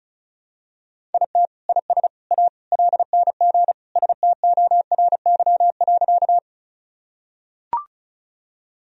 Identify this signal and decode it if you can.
Morse code, heard as a keyed tone, about 34 wpm, 700 Hz. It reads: IT IS A LNG STORY.
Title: Sentences from Top 300 Words and Abbreviations 35wpm